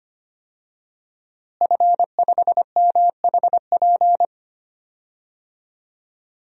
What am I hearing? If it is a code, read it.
Morse code, heard as a keyed tone, about 25 wpm, 700 Hz. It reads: F5MHP
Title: DXCC-100 Call Signs - 5 Characters - Repeated 2x - 25wpm